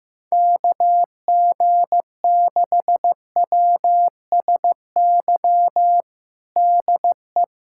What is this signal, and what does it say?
Morse code, heard as a keyed tone, about 15 wpm, 700 Hz. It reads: KG6WSY DE
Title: USA Callsign DE Callsign - 6 Characters - 15wpm